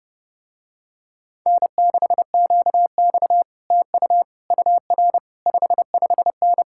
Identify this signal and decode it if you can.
Morse code, heard as a keyed tone, about 31 wpm, 700 Hz. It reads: N6QX TU UR 55N